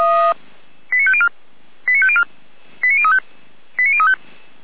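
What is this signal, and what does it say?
Five-tone selective call (ZVEI), heard as quick runs of five tones, tones of about 70 ms. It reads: 89504 89504 89035 89035
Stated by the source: ZVEI five-tone selective call 89160 received on an SDR.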